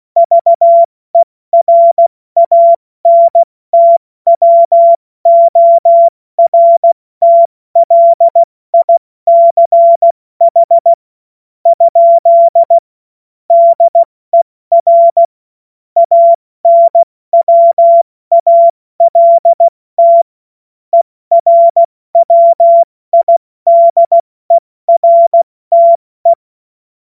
Morse code, heard as a keyed tone, about 16 wpm, spaced as for 14 wpm, 680 Hz. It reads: VERANTWORTLICH? DER ANWALT ERWIDERTE
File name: Morsekurs / Witze / Witz 04. Oktober 2022 70 F80